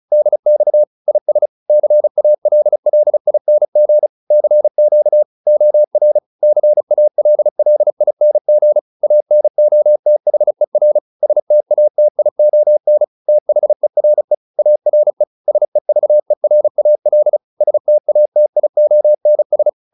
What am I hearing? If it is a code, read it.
Morse code, heard as a keyed tone, about 34 wpm, 600 Hz. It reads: DX IS CALLING CQ OR CALLING ANOTHER STATION THERE ARE SEVERAL STATIONS